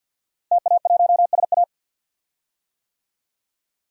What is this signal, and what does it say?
Morse code, heard as a keyed tone, about 52 wpm, 700 Hz. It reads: TA1SA
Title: DXCC-100 Call Signs - 5 Characters - Repeated 2x - 50wpm